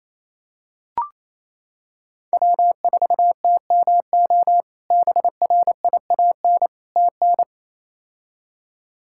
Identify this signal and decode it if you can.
Morse code, heard as a keyed tone, about 28 wpm, 700 Hz. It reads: W4TMO BRIAN TN
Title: K1USN SST QSO Practice - 28wpm